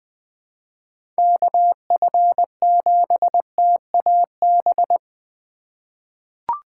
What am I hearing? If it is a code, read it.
Morse code, heard as a keyed tone, about 20 wpm, 700 Hz. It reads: KF7TAB
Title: USA Call Signs - 6 Characters - 4x Character Spacing - 20wpm